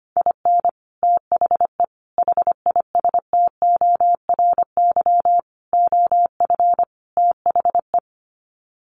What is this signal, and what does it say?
Morse code, heard as a keyed tone, about 25 wpm, 700 Hz. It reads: IN THE HISTORY OF THE
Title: Sets of 5 Words 25wpm